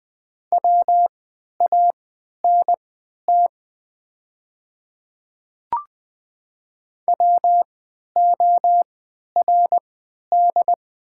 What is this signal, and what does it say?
Morse code, heard as a keyed tone, about 21 wpm, spaced as for 12 wpm, 700 Hz. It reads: WANT WORD
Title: 3x Character Spacing - Top 200 Words - 20wpm